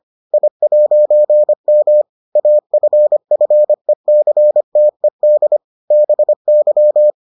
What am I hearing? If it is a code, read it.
Morse code, heard as a keyed tone, about 25 wpm, 600 Hz. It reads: I'M AFFECTED BY